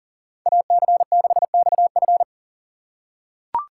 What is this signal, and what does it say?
Morse code, heard as a keyed tone, about 40 wpm, 700 Hz. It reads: AC6XF
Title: USA Call Signs - 4x Character Spacing - 40wpm